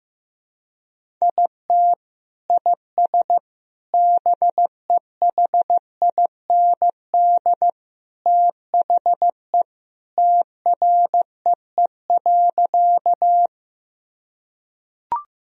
Morse code, heard as a keyed tone, about 15 wpm, 700 Hz. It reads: IT IS BEHIND THE TREE.
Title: Sentences from Top 1000 Words - Review Part 1 - 15wpm